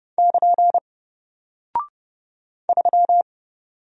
Morse code, heard as a keyed tone, about 30 wpm, 700 Hz. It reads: <KN> 3
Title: Single Character (Letter:Number:Punctuation:Prosign) Alt 30wpm